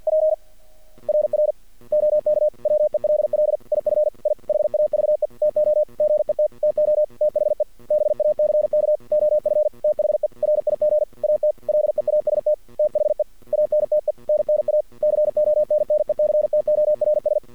Morse code, heard as a keyed tone, about 50 wpm, 610 Hz. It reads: P RR OK WELL IF I HAVE TO GET TO THE BACK OF THE RIG AT LEAST THE ANTENNA CONNECTORS
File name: archived / CRSnet